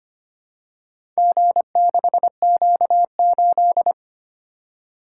Morse code, heard as a keyed tone, about 25 wpm, 700 Hz. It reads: G6Q8
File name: practice / 25wpm / Four Letter-Number Combinations - 25wpm